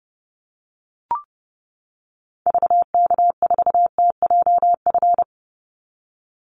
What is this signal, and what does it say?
Morse code, heard as a keyed tone, about 28 wpm, 700 Hz. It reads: VK4TJF
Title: Call Signs - 30wpm